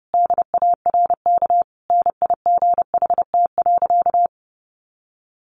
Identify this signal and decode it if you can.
Morse code, heard as a keyed tone, about 30 wpm, 700 Hz. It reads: DARK NIGHT.
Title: Sentences from Top 500 Words 30wpm